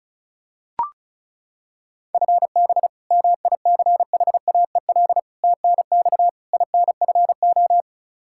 Morse code, heard as a keyed tone, about 36 wpm, 700 Hz. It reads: FB MICHAEL TNX INFO
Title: Practice QSO - Exchange 3 - More Info - 35wpm